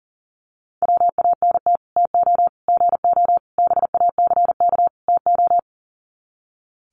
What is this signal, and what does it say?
Morse code, heard as a keyed tone, about 40 wpm, 700 Hz. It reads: WANT TO GO BACK TO
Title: Sets of 5 Words 40wpm